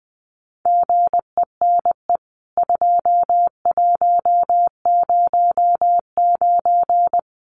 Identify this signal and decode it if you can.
Morse code, heard as a keyed tone, about 20 wpm, 700 Hz. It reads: GENE 2109
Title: CWT Exchange 20wpm